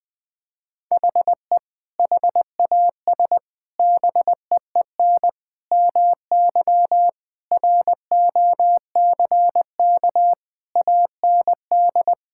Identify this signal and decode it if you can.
Morse code, heard as a keyed tone, about 20 wpm, 700 Hz. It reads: HE HAS BEEN MY ROCK AND
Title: Sentences from Top 1000 Words - Review Part 5 - 20wpm